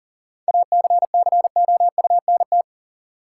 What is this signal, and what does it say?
Morse code, heard as a keyed tone, about 40 wpm, 700 Hz. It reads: ACCOUNT
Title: Sentences from Top 900 Words - New Word Drill - 40wpm